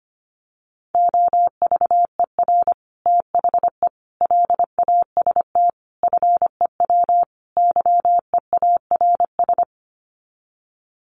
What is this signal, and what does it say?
Morse code, heard as a keyed tone, about 25 wpm, 690 Hz. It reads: OVER THE LAST FEW YEARS
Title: Sets of 5 Words Encore 25wpm